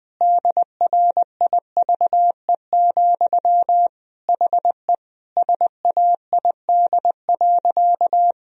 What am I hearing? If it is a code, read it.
Morse code, heard as a keyed tone, about 21 wpm, 700 Hz. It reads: DRIVE, HE SAID.
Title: Sentences from Top 1000 Words - Review Part 2 - 20wpm